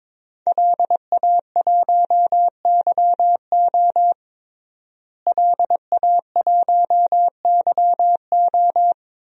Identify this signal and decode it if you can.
Morse code, heard as a keyed tone, about 22 wpm, 700 Hz. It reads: LA1YO LA1YO
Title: DXCC-100 Call Signs - 5 Characters - Repeated 3x - 22wpm